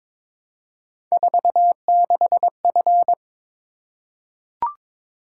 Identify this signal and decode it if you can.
Morse code, heard as a keyed tone, about 22 wpm, 700 Hz. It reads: 46F
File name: practice / 22wpm / Three Letter-Number Combinations - 22wpm